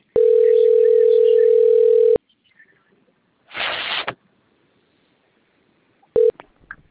Ringback tone (North America), 440 + 480 Hz.
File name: out- - - - - .98601